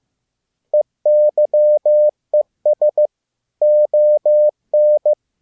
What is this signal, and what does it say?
Morse code, heard as a keyed tone, about 15 wpm, 600 Hz. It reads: EYES ON